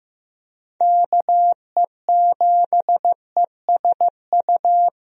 Morse code, heard as a keyed tone, about 15 wpm, 700 Hz. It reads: KE7ESU
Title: USA Call Signs - 6 Characters - 15wpm